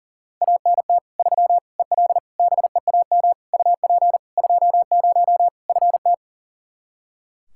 Morse code, heard as a keyed tone, about 40 wpm, 700 Hz. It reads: ANT 3 EL BEAM UP 20 FT
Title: Practice QSO - Exchange 2 - Info - 40wpm